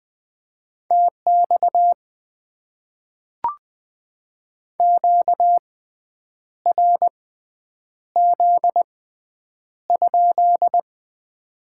Morse code, heard as a keyed tone, about 20 wpm, spaced as for 8 wpm, 700 Hz. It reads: TX QRZ?
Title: Top 90 QSO Elements - 6x Character Spacing - 20wpm